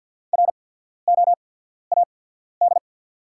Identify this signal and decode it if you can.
Morse code, heard as a keyed tone, about 54 wpm, spaced as for 14 wpm, 700 Hz. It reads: ROAD
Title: 8x Character Spacing - Top 500 Words - 50wpm